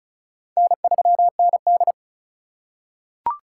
Morse code, heard as a keyed tone, about 34 wpm, 700 Hz. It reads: N3ND
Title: K1USN SST QSO Practice - 35wpm